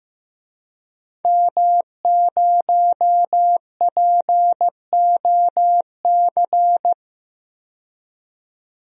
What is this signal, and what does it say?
Morse code, heard as a keyed tone, about 15 wpm, 700 Hz. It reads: M0POC
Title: DXCC-100 Call Signs - 5 Characters - 15wpm